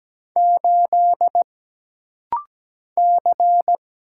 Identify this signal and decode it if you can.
Morse code, heard as a keyed tone, about 17 wpm, 700 Hz. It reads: 8 C
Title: Single Letter-Number - Rapid-Fire 17wpm